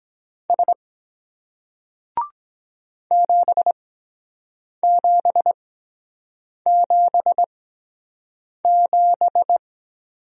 Morse code, heard as a keyed tone, about 22 wpm, spaced as for 7 wpm, 700 Hz. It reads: S 7777